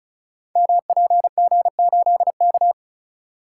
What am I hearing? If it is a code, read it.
Morse code, heard as a keyed tone, about 36 wpm, 700 Hz. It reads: MPG8K
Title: Code Groups - 5 Letter:Numbers - 35wpm